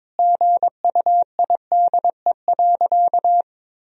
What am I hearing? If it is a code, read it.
Morse code, heard as a keyed tone, about 22 wpm, 700 Hz. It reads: GUIDE.
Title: Sentences from Top 1000 Words - Review Part 3 - 22wpm